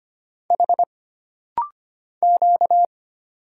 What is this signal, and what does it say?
Morse code, heard as a keyed tone, about 25 wpm, 700 Hz. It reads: H Q
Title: Single Letters - Rapid-Fire 25wpm